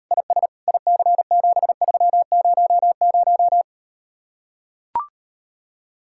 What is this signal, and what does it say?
Morse code, heard as a keyed tone, about 39 wpm, 700 Hz. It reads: IS IC7300